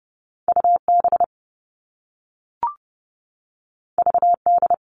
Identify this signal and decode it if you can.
Morse code, heard as a keyed tone, about 30 wpm, 700 Hz. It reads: U B V D
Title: All Cut Numbers - 2-Digit 30wpm